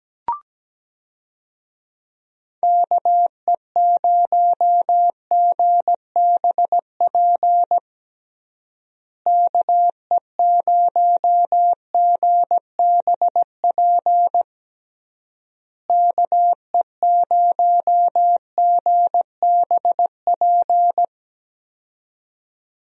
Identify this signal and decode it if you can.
Morse code, heard as a keyed tone, about 17 wpm, 700 Hz. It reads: KE0GBP KE0GBP KE0GBP
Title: USA Call Signs - 6 Characters - Repeated 3x - 17wpm